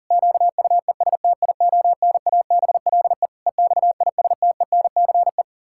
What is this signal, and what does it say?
Morse code, heard as a keyed tone, about 37 wpm, 700 Hz. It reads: QUESTIONABLE EXISTENCE
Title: Pangram Challenge v2 - 40wpm